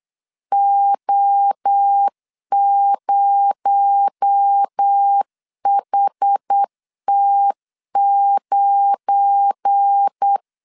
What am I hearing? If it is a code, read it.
Morse code, heard as a keyed tone, about 8 wpm, 800 Hz. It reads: O0HT9
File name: LEZIONI / 8 YJ45 / 8 Y 8wpm 3245